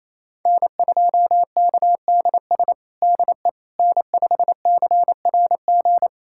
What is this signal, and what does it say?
Morse code, heard as a keyed tone, about 28 wpm, 700 Hz. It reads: N2KDS DE N5CRG
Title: USA Callsign DE Callsign - 5 Characters - 28wpm